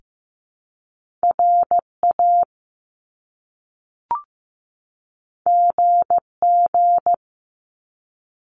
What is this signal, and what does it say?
Morse code, heard as a keyed tone, about 15 wpm, 700 Hz. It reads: RA GG
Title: Top Two Letter Combinations in Words 15wpm